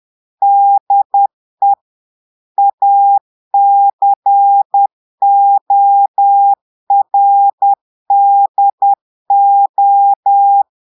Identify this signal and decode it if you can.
Morse code, heard as a keyed tone, about 10 wpm, 800 Hz. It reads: DE ACORDO
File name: portugais texte en clair 2 vitesse 10 mots minutes